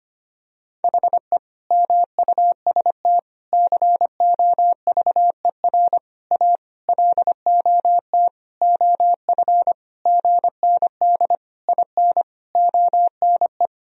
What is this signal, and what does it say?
Morse code, heard as a keyed tone, about 25 wpm, 700 Hz. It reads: HE MUST COVER A LOT OF GND IN ONE
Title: Sentences from Top 500 Words and Abbreviations 25wpm